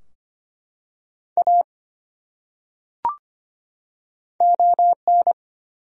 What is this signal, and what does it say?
Morse code, heard as a keyed tone, about 25 wpm, 700 Hz. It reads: A ON
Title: Top 100 Words 25wpm